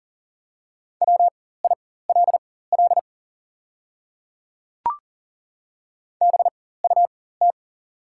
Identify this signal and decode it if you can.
Morse code, heard as a keyed tone, about 40 wpm, spaced as for 19 wpm, 700 Hz. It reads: WILL BUT
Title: 4x Character Spacing - Top 100 Words - 40wpm